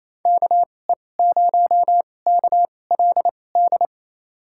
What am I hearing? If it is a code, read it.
Morse code, heard as a keyed tone, about 27 wpm, spaced as for 20 wpm, 700 Hz. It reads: KE0KLD
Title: USA Call Signs - 6 Characters - 2x Character Spacing - 28wpm